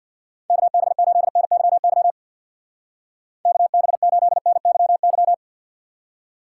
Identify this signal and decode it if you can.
Morse code, heard as a keyed tone, about 52 wpm, 700 Hz. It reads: KB8NYY KB8NYY
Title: Speed Racing - USA 2x3 Call Signs 50wpm